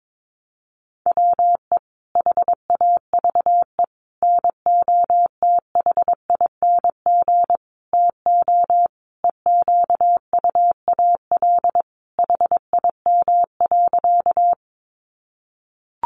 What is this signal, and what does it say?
Morse code, heard as a keyed tone, about 22 wpm, 700 Hz. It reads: WE HAVE NOTHING TO EQUAL HIM.
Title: Sentences from Top 1400 Words 22wpm